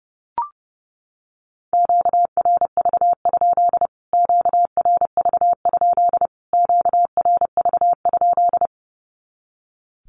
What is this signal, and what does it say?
Morse code, heard as a keyed tone, about 30 wpm, 700 Hz. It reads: QRV? QRV? QRV?
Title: Q-Codes Repeated 3x 30wpm